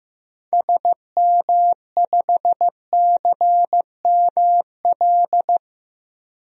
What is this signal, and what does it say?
Morse code, heard as a keyed tone, about 16 wpm, 700 Hz. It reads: SM5CML
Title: DXCC-100 Call Signs - 6 Characters - Repeated 3x - 15wpm